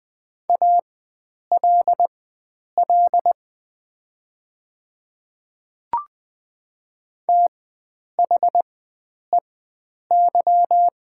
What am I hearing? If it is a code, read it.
Morse code, heard as a keyed tone, about 20 wpm, spaced as for 9 wpm, 700 Hz. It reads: ALL THEY
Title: 4x Character Spacing - Top 100 Words - 20wpm